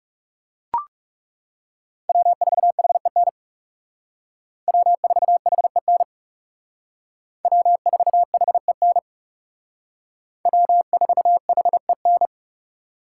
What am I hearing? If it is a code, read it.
Morse code, heard as a keyed tone, about 36 wpm, 700 Hz. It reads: W4HEN W4HEN W4HEN W4HEN
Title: Speed Racing - USA 1x3 Call Signs 30wpm